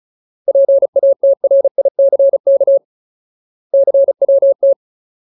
Morse code, heard as a keyed tone, about 34 wpm, 550 Hz. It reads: PATRICK CWT
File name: Ultimate Word List / 35WPM Ultimate